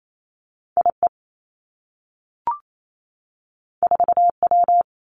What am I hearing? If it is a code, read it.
Morse code, heard as a keyed tone, about 28 wpm, 700 Hz. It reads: I E 4 W
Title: Two Challenging Letter-Number Combinations 28wpm